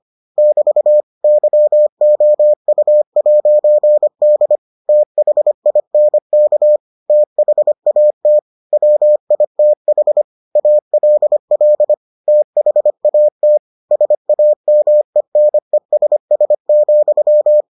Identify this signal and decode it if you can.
Morse code, heard as a keyed tone, about 25 wpm, 600 Hz. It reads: <BT> YOU'D THINK THAT WITH ALL THAT SAMENESS,